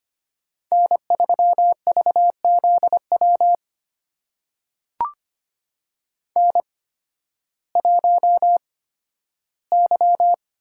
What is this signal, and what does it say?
Morse code, heard as a keyed tone, about 25 wpm, spaced as for 13 wpm, 700 Hz. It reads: N3VZW N1Y
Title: USA Call Signs - 5 Characters - 8x Character Spacing - 25wpm